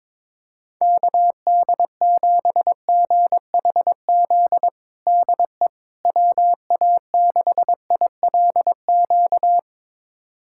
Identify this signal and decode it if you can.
Morse code, heard as a keyed tone, about 22 wpm, 700 Hz. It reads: KD7GHZ DE WA6ILQ
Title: USA Callsign DE Callsign - 6 Characters - 22wpm